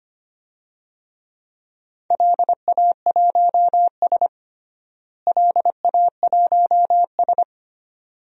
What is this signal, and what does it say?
Morse code, heard as a keyed tone, about 25 wpm, 700 Hz. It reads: LA1S LA1S
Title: DXCC-100 Call Signs - 4 Characters - Repeated 2x - 25wpm